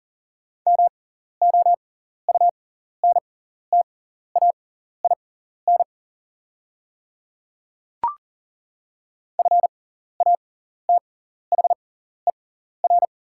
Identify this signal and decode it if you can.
Morse code, heard as a keyed tone, about 40 wpm, spaced as for 14 wpm, 690 Hz. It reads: MOUNTAIN FATHER
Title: 6x Character Spacing - Top 500 Words - 40wpm